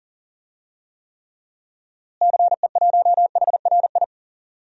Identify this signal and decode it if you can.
Morse code, heard as a keyed tone, about 40 wpm, 700 Hz. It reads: CE1HRI
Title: DXCC-100 Call Signs - 6 Characters - Repeated 2x - 40wpm